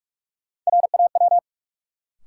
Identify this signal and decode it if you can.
Morse code, heard as a keyed tone, about 43 wpm, 700 Hz. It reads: RAW
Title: Sentences from Top 1500 Words - New Word Drill Alt - 45wpm